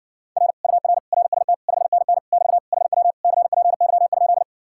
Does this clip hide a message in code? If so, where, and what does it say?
Yes: Morse code from 0.4 to 4.4 s, heard as a keyed tone, about 60 wpm, 700 Hz. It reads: R UR RST 5NN <BT> HW CPY?